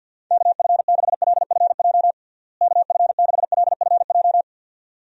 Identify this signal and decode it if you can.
Morse code, heard as a keyed tone, about 52 wpm, 700 Hz. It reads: KF6LFJ KF6LFJ
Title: USA Call Signs - 6 Characters - Repeated 2x - 50wpm